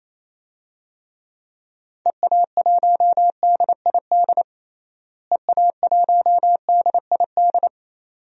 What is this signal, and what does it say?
Morse code, heard as a keyed tone, about 28 wpm, 700 Hz. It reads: EA1DID EA1DID
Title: DXCC-100 Call Signs - 6 Characters - Repeated 2x - 28wpm